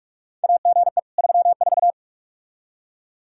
Morse code, heard as a keyed tone, about 43 wpm, 700 Hz. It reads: AGE 34
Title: K1USN SST QSO Practice 45wpm